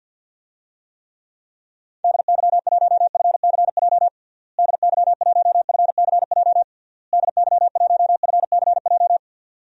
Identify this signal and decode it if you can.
Morse code, heard as a keyed tone, about 53 wpm, 700 Hz. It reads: DY1FCJ DY1FCJ DY1FCJ
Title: DXCC-100 Call Signs - 6 Characters - Repeated 3x - 50wpm